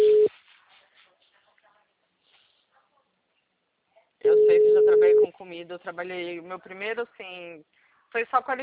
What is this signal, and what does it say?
Ringback tone (Europe, 425 Hz), with speech.